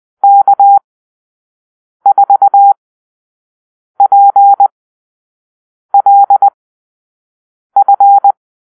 Morse code, heard as a keyed tone, about 20 wpm, spaced as for 6 wpm, 800 Hz. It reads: K4PLF
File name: arrl / Call-Sign Drill-5